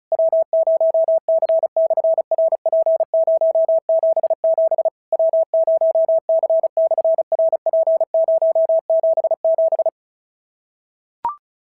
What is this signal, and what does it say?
Morse code, heard as a keyed tone, about 34 wpm, 650 Hz. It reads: W0C/RP077 W0C/RP077
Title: SOTA QSO Practice 35wpm